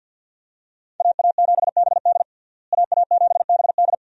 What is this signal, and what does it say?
Morse code, heard as a keyed tone, about 46 wpm, 700 Hz. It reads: AA7BD AA7BD
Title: USA Call Signs - 5 Characters - Repeated 2x - 50wpm